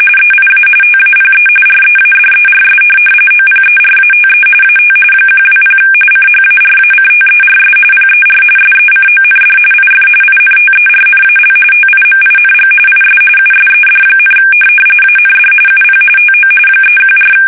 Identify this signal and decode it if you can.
RTTY, about 45.45 baud. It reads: EIA TAPE READERS OPERATE AT 120 CHARACTERS PER SECOND IF INTERNAL TIMING IS USED, AND UP TO 314
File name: fsk850 / teletype-corp